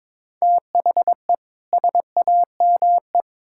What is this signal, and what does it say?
Morse code, heard as a keyed tone, about 22 wpm, 700 Hz. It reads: THE SAME
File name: Build A Phrase - 2-Word Phrases - 22wpm